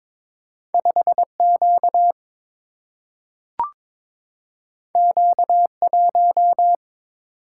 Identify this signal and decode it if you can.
Morse code, heard as a keyed tone, about 22 wpm, 700 Hz. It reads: 5 Q Q 1
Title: Single-Letter Single-Number 22wpm